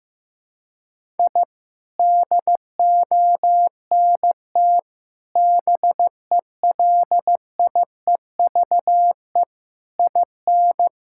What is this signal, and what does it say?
Morse code, heard as a keyed tone, about 15 wpm, 700 Hz. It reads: I DONT BELIEVE IN
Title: Sentences from Top 1000 Words - Review Part 6 - 15wpm